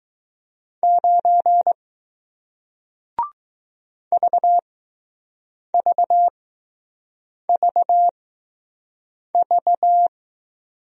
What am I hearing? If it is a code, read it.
Morse code, heard as a keyed tone, about 21 wpm, spaced as for 6 wpm, 700 Hz. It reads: 9 VVVV